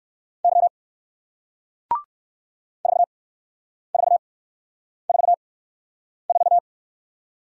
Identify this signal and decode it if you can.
Morse code, heard as a keyed tone, about 58 wpm, spaced as for 9 wpm, 700 Hz. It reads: Y 4444